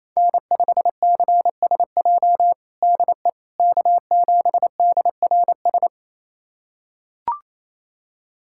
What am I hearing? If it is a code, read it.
Morse code, heard as a keyed tone, about 27 wpm, 700 Hz. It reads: N5CSJ DE K7DRS
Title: USA Callsign DE Callsign - 5 Characters - 28wpm